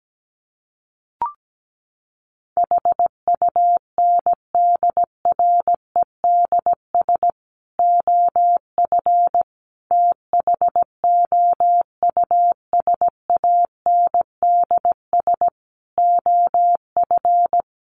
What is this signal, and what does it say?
Morse code, heard as a keyed tone, about 17 wpm, 700 Hz. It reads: HUNDREDS OF THOUSANDS OF
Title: Sets of 5 Words Encore 17wpm